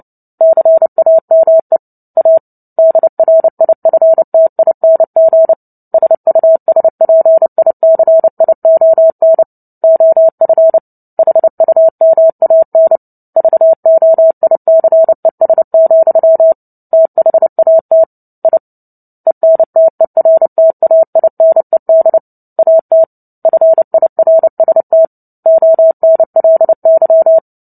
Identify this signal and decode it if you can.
Morse code, heard as a keyed tone, about 30 wpm, 650 Hz. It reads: CAME A DRIFTING SUSPICION OF HUMAN VOICES, THAT I ENTERTAINED AT FIRST ONLY
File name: Book Two / 03-Chapter 3 - 29 WPM